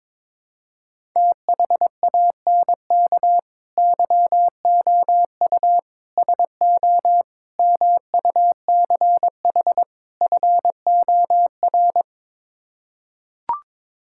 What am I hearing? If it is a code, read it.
Morse code, heard as a keyed tone, about 22 wpm, 700 Hz. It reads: THANK YOU SO MUCH FOR
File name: practice / 22wpm / Build A Phrase - 5-Word Phrases - 22wpm